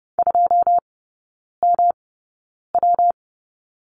Morse code, heard as a keyed tone, about 30 wpm, spaced as for 9 wpm, 700 Hz. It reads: 2MW
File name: Lesson 19e - Letter w - Three Characters 30wpm